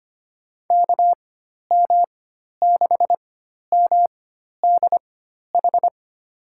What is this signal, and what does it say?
Morse code, heard as a keyed tone, about 25 wpm, spaced as for 12 wpm, 700 Hz. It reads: KM6MDH
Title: USA Call Signs - 6 Characters - 4x Character Spacing - 25wpm